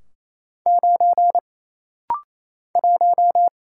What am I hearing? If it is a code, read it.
Morse code, heard as a keyed tone, about 28 wpm, 700 Hz. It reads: 9 1
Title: Single Letter-Number Alt - Rapid-Fire 28wpm